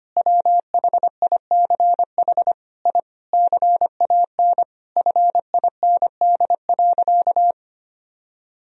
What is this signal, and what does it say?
Morse code, heard as a keyed tone, about 25 wpm, 700 Hz. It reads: WHICH I CAN FIND.
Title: Sentences from Top 1000 Words - Review Part 2 - 25wpm